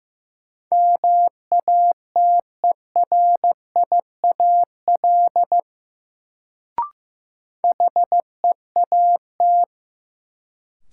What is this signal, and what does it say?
Morse code, heard as a keyed tone, about 16 wpm, 700 Hz. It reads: MATERIAL HEAT
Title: Sentences from Top 1100 Words - New Word Drill Alt - 15wpm